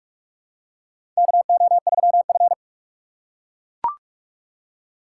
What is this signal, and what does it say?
Morse code, heard as a keyed tone, about 43 wpm, 700 Hz. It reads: KO3F
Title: USA Call Signs - 4 Characters - 6x Character Spacing - 45wpm